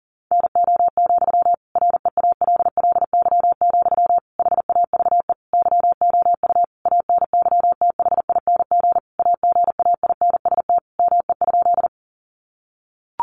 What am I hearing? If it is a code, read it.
Morse code, heard as a keyed tone, about 37 wpm, 700 Hz. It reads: NO, REALLY, HAVE YOU ANYTHING AGAINST ME?
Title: Sentences from Top 1000 Words 40wpm